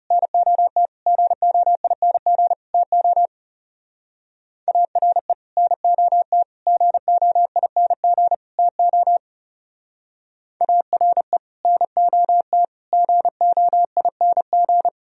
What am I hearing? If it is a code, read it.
Morse code, heard as a keyed tone, about 36 wpm, 700 Hz. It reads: NOT GOING TO ARE NOT GOING TO ARE NOT GOING
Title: Speed-Racing - Sets of 4 Words 30wpm